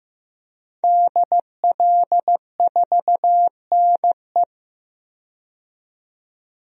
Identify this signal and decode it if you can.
Morse code, heard as a keyed tone, about 15 wpm, 700 Hz. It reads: DL4NE
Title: DXCC-100 Call Signs - 5 Characters - 15wpm